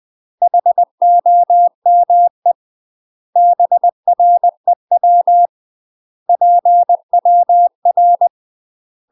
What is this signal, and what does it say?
Morse code, heard as a keyed tone, about 20 wpm, 700 Hz. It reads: HOME BREW PWR
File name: UK MASTER QSOs TRAINING / 20WPM / UK MASTER QSOs 0068